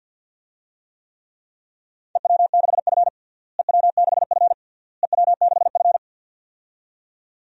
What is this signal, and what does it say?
Morse code, heard as a keyed tone, about 46 wpm, 700 Hz. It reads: EW6F EW6F EW6F
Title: DXCC-100 Call Signs - 4 Characters - Repeated 3x - 50wpm